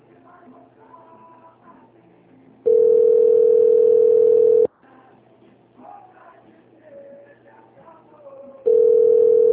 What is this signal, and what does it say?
Ringback tone (North America), 440 + 480 Hz.
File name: rg-1600-unknown- - - .55096